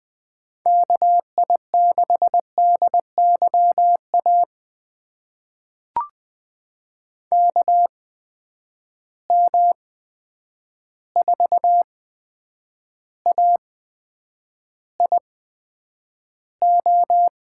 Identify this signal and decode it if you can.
Morse code, heard as a keyed tone, about 20 wpm, spaced as for 9 wpm, 700 Hz. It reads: KI6DYA KM4AIO
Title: USA Call Signs - 6 Characters - 8x Character Spacing - 20wpm